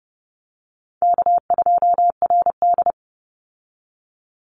Morse code, heard as a keyed tone, about 32 wpm, 700 Hz. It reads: K2RD
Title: Call Signs - 30wpm